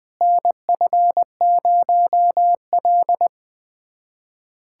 Morse code, heard as a keyed tone, about 20 wpm, 700 Hz. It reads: NF0L
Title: USA Callsign DE Callsign - 4 Characters - 20wpm